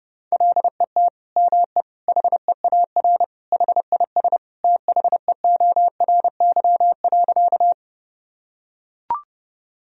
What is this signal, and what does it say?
Morse code, heard as a keyed tone, about 32 wpm, 700 Hz. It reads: LET ME HEAR HIS THEORY.
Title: Sentences from Top 1000 Words - Review Part 3 - 30wpm